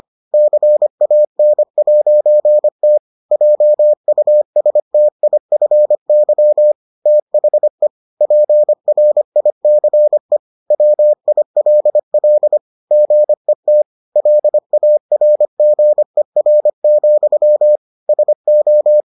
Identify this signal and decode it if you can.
Morse code, heard as a keyed tone, about 25 wpm, 600 Hz. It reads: CAN'T JUSTIFY THE PRICE WILL GET LARGER, SO